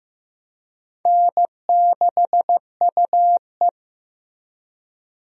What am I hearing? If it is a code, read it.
Morse code, heard as a keyed tone, about 15 wpm, 700 Hz. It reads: N6UE
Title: USA Call Signs - 4 Characters - 15wpm